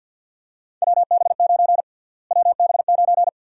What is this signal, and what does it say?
Morse code, heard as a keyed tone, about 52 wpm, 700 Hz. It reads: WB9 WB9
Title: USA Call Sign Prefixes - Repeated 2x - 50wpm